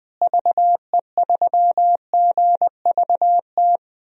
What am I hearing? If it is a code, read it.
Morse code, heard as a keyed tone, about 21 wpm, 700 Hz. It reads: VE3GVT
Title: DXCC-100 Call Signs - 6 Characters - Repeated 3x - 20wpm